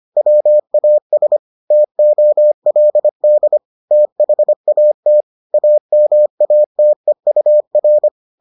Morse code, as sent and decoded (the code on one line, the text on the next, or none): .-- .- ... / - --- .-.. -.. / - .... .- - / .- -- .- - . ..- .-.
WAS TOLD THAT AMATEUR